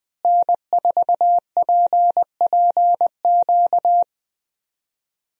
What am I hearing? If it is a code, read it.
Morse code, heard as a keyed tone, about 20 wpm, 700 Hz. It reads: N4PPQ
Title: USA Call Signs - 5 Characters - Repeated 3x - 20wpm